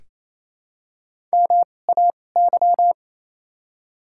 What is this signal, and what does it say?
Morse code, heard as a keyed tone, about 28 wpm, 700 Hz. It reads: M A Y
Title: 2x Character Spacing - Top 200 Words - 28wpm